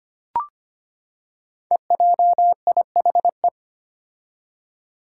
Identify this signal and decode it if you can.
Morse code, heard as a keyed tone, about 25 wpm, 700 Hz. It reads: EJIHE
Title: Five Challenging Letter-Number Combinations 25wpm